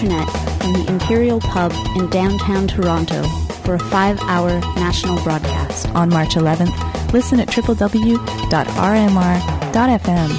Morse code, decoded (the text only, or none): none